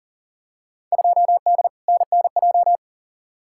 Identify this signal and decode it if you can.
Morse code, heard as a keyed tone, about 40 wpm, 700 Hz. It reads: 2D NNJ
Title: Field Day Exchanges - 40wpm